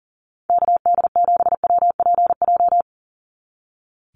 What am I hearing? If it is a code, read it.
Morse code, heard as a keyed tone, about 40 wpm, 700 Hz. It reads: KD7WPJ
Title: NA SOTA Activator Callsigns 40wpm